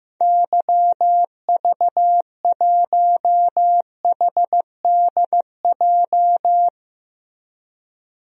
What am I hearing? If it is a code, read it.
Morse code, heard as a keyed tone, about 15 wpm, 700 Hz. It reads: YV1HDJ
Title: DXCC-100 Call Signs - 6 Characters - 15wpm